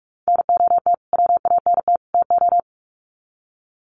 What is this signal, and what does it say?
Morse code, heard as a keyed tone, about 44 wpm, 700 Hz. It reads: NOT WANT TO
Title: Sets of 3 Words 45wpm